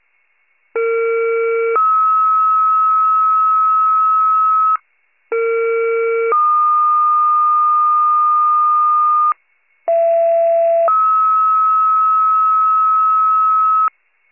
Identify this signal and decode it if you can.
Two-tone page, heard as a one-second tone, then a three-second tone, about 460 then 1290 Hz, about 460 then 1220 Hz, about 670 then 1290 Hz. It is heard from 0.8 to 13.9 s.
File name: pager / 21 RESCUE / wcalert